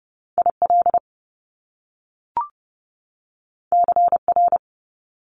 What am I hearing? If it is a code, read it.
Morse code, heard as a keyed tone, about 30 wpm, 700 Hz. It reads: IL CR
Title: Top Two Letter Combinations in Words 30wpm